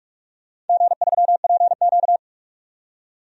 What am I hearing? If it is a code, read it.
Morse code, heard as a keyed tone, about 45 wpm, 700 Hz. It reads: G3PQ
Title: DXCC-100 Call Signs - 4 Characters - 45wpm